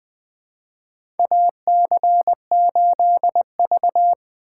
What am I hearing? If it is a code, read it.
Morse code, heard as a keyed tone, about 20 wpm, 700 Hz. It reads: AC8V
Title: USA Call Signs - 3x Character Spacing - 20wpm